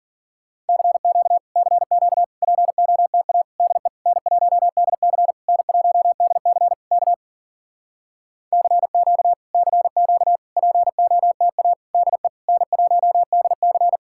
Morse code, heard as a keyed tone, about 52 wpm, 700 Hz. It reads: CQ CQ POTA DE N1DC N1DC K CQ CQ POTA DE N1DC